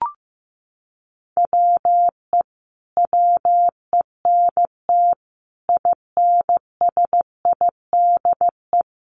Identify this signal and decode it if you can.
Morse code, heard as a keyed tone, about 15 wpm, 700 Hz. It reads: WE WENT INSIDE
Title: Sentences from Top 1200 Words 15wpm